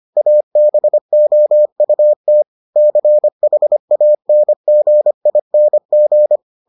Morse code, heard as a keyed tone, about 25 wpm, 600 Hz. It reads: ABOUT CHANGING